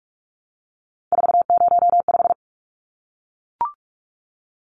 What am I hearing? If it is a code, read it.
Morse code, heard as a keyed tone, about 44 wpm, 700 Hz. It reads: 405
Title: Numbers - 3-Digit 45wpm